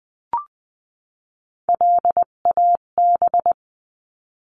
Morse code, heard as a keyed tone, about 20 wpm, 700 Hz. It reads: LAB
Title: 258 Lesson 42c - Words 16-20wpm